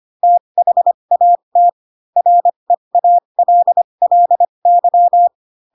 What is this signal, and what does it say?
Morse code, heard as a keyed tone, about 25 wpm, spaced as for 22 wpm, 700 Hz. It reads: THAT REALLY
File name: Pinocchio / Pinocchio ch4 18 25 ik1hge